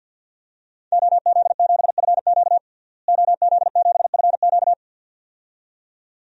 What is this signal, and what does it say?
Morse code, heard as a keyed tone, about 53 wpm, 700 Hz. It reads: OZ7FQ OZ7FQ
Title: DXCC-100 Call Signs - 5 Characters - Repeated 2x - 50wpm